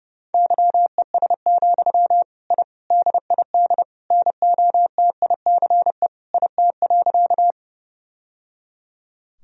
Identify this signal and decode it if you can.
Morse code, heard as a keyed tone, about 30 wpm, 700 Hz. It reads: YES, I DID NOTICE IT.
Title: Sentences from Top 1000 Words - Review Part 2 - 30wpm